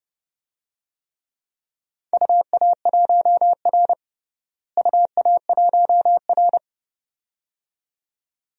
Morse code, heard as a keyed tone, about 30 wpm, 700 Hz. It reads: UA1R UA1R
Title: DXCC-100 Call Signs - 4 Characters - Repeated 2x - 30wpm